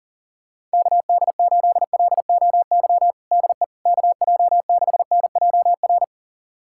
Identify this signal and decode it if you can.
Morse code, heard as a keyed tone, about 40 wpm, 700 Hz. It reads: KD8LOY DE KJ6NJR
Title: USA Callsign DE Callsign - 6 Characters - 40wpm